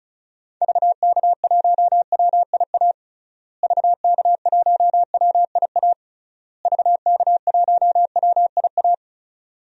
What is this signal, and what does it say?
Morse code, heard as a keyed tone, about 35 wpm, 700 Hz. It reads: VK1WIA VK1WIA VK1WIA
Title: DXCC-100 Call Signs - 6 Characters - Repeated 3x - 35wpm